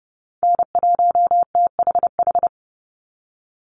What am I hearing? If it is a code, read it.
Morse code, heard as a keyed tone, about 30 wpm, 700 Hz. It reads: N1THH
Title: USA Call Signs 30wpm